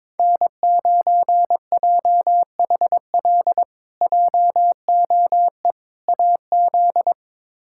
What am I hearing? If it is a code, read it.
Morse code, heard as a keyed tone, about 22 wpm, 700 Hz. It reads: N9JHL JOE AZ
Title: K1USN SST QSO Practice - 22wpm